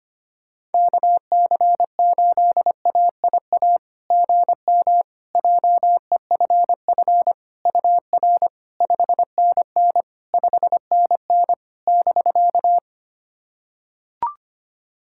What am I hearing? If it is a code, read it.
Morse code, heard as a keyed tone, about 25 wpm, 700 Hz. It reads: KC8AIA GM JEFF UR 5NN 5NN <BK>
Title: SOTA QSO Practice 25wpm